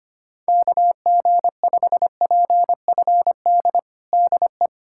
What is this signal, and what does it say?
Morse code, heard as a keyed tone, about 25 wpm, 700 Hz. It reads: KG5PFD DE
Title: USA Callsign DE Callsign 25wpm